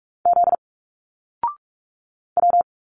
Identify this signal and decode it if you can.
Morse code, heard as a keyed tone, about 46 wpm, 700 Hz. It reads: Z W
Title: Single Letters 45wpm